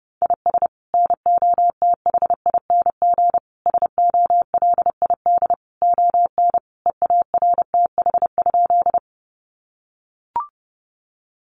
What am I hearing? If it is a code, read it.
Morse code, heard as a keyed tone, about 30 wpm, 700 Hz. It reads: IS NOTHING SOLID ON EARTH?
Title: Sentences from Top 1500 Words 30wpm